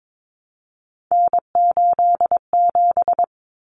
Morse code, heard as a keyed tone, about 22 wpm, 700 Hz. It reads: N87
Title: Common Cut Numbers - 3-Digit 22wpm